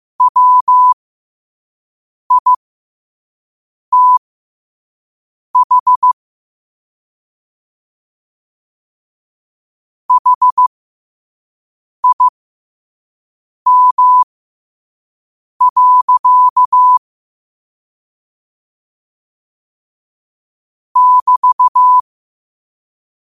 Morse code, heard as a keyed tone, about 15 wpm, spaced as for 5 wpm, 1000 Hz. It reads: WITH HIM. <BT>
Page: New quotes every day in morse code at 5 Words per minute.